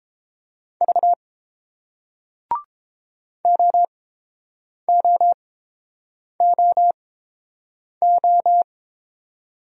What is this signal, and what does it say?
Morse code, heard as a keyed tone, about 30 wpm, spaced as for 7 wpm, 700 Hz. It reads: V OOOO